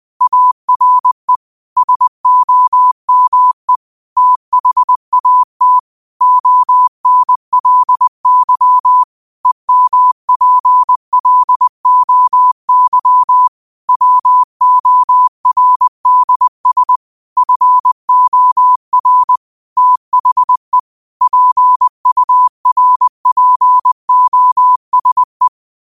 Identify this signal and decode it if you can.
Morse code, heard as a keyed tone, about 20 wpm, 1000 Hz. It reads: ARE SOME THAT ONLY EMPLOY WORDS FOR THE PURPOSE
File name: qotdfiles / 20wpm